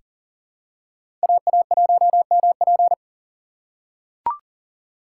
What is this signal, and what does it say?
Morse code, heard as a keyed tone, about 40 wpm, 700 Hz. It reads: AA1MP
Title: USA Call Signs - 5 Characters - 4x Character Spacing - 40wpm